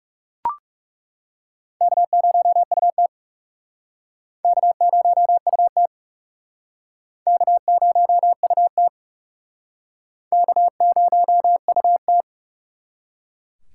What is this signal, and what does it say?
Morse code, heard as a keyed tone, about 36 wpm, 700 Hz. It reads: K0UT K0UT K0UT K0UT
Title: Speed Racing - USA 1x2 Call Signs 30wpm